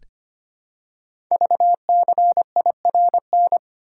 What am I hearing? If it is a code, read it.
Morse code, heard as a keyed tone, about 25 wpm, 700 Hz. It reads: VCIRN
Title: Code Groups - 5 Letters - 25wpm